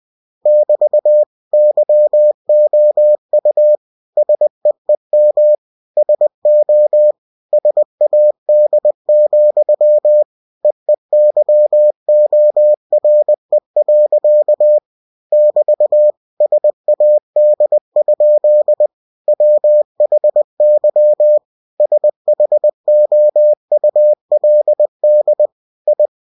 Morse code, heard as a keyed tone, about 20 wpm, 600 Hz. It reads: <BT> YOU SEEM SO SAD, EEYORE. <BT> SAD? WHY SHOULD I
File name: cw-books / winnie-the-pooh / winnie-20wpm-page-0046